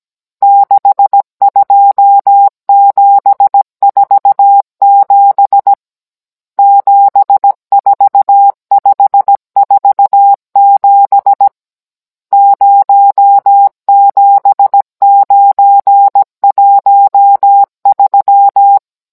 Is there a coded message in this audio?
Yes: Morse code from 0.4 to 18.8 s, heard as a keyed tone, about 17 wpm, 800 Hz. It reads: 62747 74547 07913